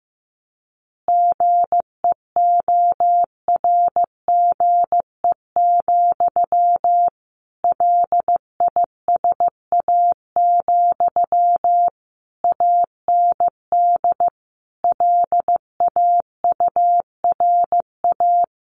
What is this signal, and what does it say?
Morse code, heard as a keyed tone, about 15 wpm, 700 Hz. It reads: GEORGE, LISA, AND LAURA
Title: Additional Easy Phrases 15wpm